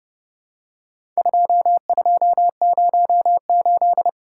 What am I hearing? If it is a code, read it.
Morse code, heard as a keyed tone, about 30 wpm, 700 Hz. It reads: 2208
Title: Worked All Europe DX Contest - QTC Exchange - 30wpm